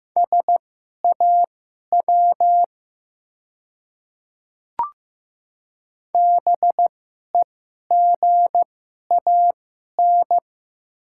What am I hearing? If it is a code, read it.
Morse code, heard as a keyed tone, about 15 wpm, spaced as for 11 wpm, 700 Hz. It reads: SAW BEGAN
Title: 2x Character Spacing - Top 300 Words - 15wpm